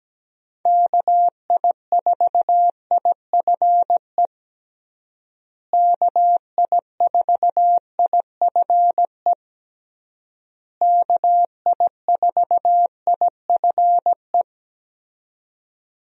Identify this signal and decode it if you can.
Morse code, heard as a keyed tone, about 17 wpm, 700 Hz. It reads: KI4IFE KI4IFE KI4IFE
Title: USA Call Signs - 6 Characters - Repeated 3x - 17wpm